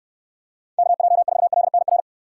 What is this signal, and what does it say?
Morse code, heard as a keyed tone, about 69 wpm, 700 Hz. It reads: KJ4LNR